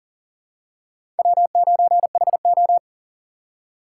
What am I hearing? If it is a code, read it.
Morse code, heard as a keyed tone, about 37 wpm, 690 Hz. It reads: W9HO
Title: USA Call Signs - 4x Character Spacing - 40wpm